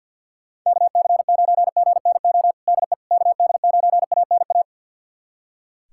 Morse code, heard as a keyed tone, about 46 wpm, 700 Hz. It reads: KC9GNO DE KD9ANA
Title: USA Callsign DE Callsign - 6 Characters - 50wpm